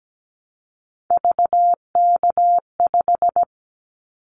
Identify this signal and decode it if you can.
Morse code, heard as a keyed tone, about 17 wpm, 700 Hz. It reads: VK5
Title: Callsign Prefixes for Top 100 DXCC - 17wpm